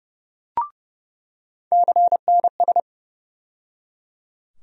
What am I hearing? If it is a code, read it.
Morse code, heard as a keyed tone, about 30 wpm, 700 Hz. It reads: CNS
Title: USA Call Sign Postfixes - 3 Letters 30wpm